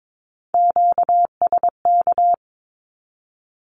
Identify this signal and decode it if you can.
Morse code, heard as a keyed tone, about 22 wpm, 700 Hz. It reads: QSK
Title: Q-Codes Repeated 3x 22wpm